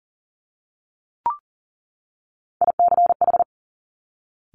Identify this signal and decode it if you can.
Morse code, heard as a keyed tone, about 38 wpm, 700 Hz. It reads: ICH
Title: Top Three Letter Combinations in Words 40wpm